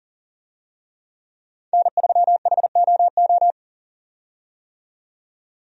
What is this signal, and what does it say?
Morse code, heard as a keyed tone, about 40 wpm, 700 Hz. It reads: N3HOO
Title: USA Call Signs - 5 Characters - 40wpm